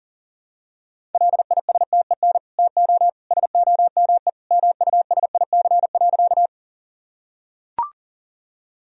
Morse code, heard as a keyed tone, about 37 wpm, 700 Hz. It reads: LISTEN TO SOME MUSIC.
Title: Sentences from Top 1000 Words - Review Part 1 - 40wpm